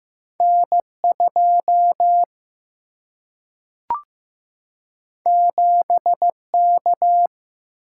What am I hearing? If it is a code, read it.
Morse code, heard as a keyed tone, about 15 wpm, 700 Hz. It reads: N2 7K
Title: Single-Letter Single-Number 15wpm